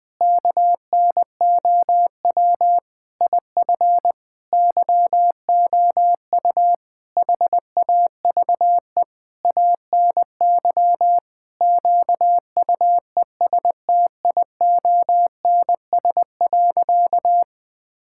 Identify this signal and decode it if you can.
Morse code, heard as a keyed tone, about 21 wpm, 700 Hz. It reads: KNOW IF YOU HAVE ANY QUESTIONS.
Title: Sentences with 7-Word Phrases - Spoken Beginning - 20wpm